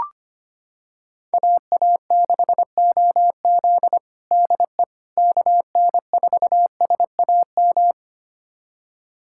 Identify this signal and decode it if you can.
Morse code, heard as a keyed tone, about 25 wpm, 700 Hz. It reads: AA6OZ DE KN4SAM
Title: USA Callsign DE Callsign 25wpm